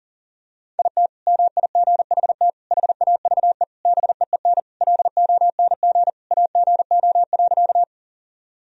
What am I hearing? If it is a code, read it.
Morse code, heard as a keyed tone, about 40 wpm, 700 Hz. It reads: IT MIGHT HAVE BEEN LONG AGO.
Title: Sentences from Top 1000 Words - Review Part 3 - 40wpm